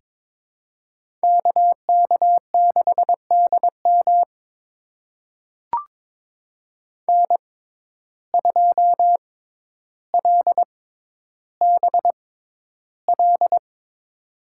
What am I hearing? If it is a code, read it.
Morse code, heard as a keyed tone, about 22 wpm, spaced as for 11 wpm, 700 Hz. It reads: KK6DM N2LBL
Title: USA Call Signs - 5 Characters - 6x Character Spacing - 22wpm